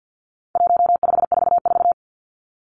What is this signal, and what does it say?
Morse code, heard as a keyed tone, about 50 wpm, 700 Hz. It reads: 1544
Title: Numbers - 4-Digit 50wpm